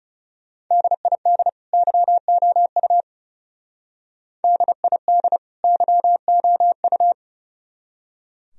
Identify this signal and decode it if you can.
Morse code, heard as a keyed tone, about 37 wpm, 700 Hz. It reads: DID YOU DID YOU
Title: Speed-Racing - Sets of 2 Words 30wpm